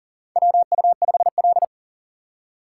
Morse code, heard as a keyed tone, about 40 wpm, 700 Hz. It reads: WU5L
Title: USA Call Signs - 4 Characters - 4x Character Spacing - 40wpm